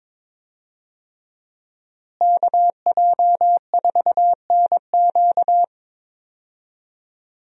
Morse code, heard as a keyed tone, about 21 wpm, 700 Hz. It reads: KJ4NQ
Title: USA Call Signs - 5 Characters - Repeated 3x - 22wpm